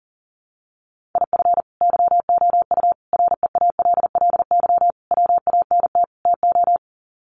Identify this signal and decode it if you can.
Morse code, heard as a keyed tone, about 40 wpm, 700 Hz. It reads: IF YOU REALLY WANT TO
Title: Sets of 5 Words Encore2 40wpm